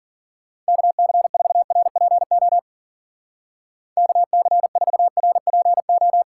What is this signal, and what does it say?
Morse code, heard as a keyed tone, about 49 wpm, 700 Hz. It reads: KC4RPO KC4RPO
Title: Speed Racing - USA 2x3 Call Signs 40wpm